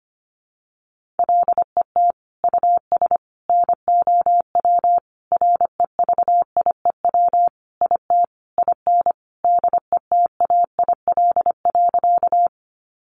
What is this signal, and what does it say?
Morse code, heard as a keyed tone, about 25 wpm, 700 Hz. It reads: LET US NOW REVIEW IT IN DETAIL.
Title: Sentences from Top 1500 Words 25wpm